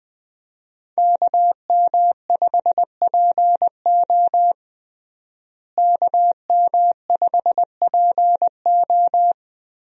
Morse code, heard as a keyed tone, about 20 wpm, 700 Hz. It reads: KM5PO KM5PO
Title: USA Call Signs - 5 Characters - Repeated 2x - 20wpm